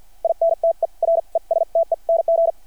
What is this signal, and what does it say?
Morse code, heard as a keyed tone, about 52 wpm, spaced as for 41 wpm, 660 Hz. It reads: INTERESTENG
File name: archived / CRSnet